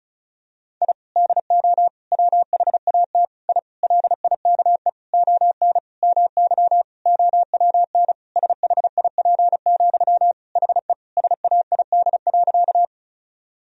Morse code, heard as a keyed tone, about 34 wpm, 700 Hz. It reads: I DO WHAT I LIKE ON MY OWN SHIP, HE SAID.
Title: Sentences from Top 1000 Words - Review Part 2 - 35wpm